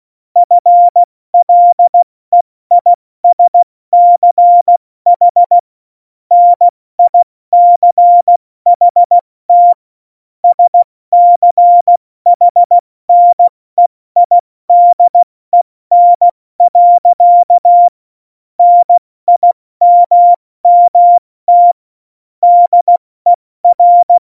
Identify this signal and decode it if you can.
Morse code, heard as a keyed tone, about 16 wpm, 700 Hz. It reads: FLEISCH NICHT SCHNEIDEN. NIMMT DER
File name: Morsekurs / Witze / Witz 26. November 2019, 70 F80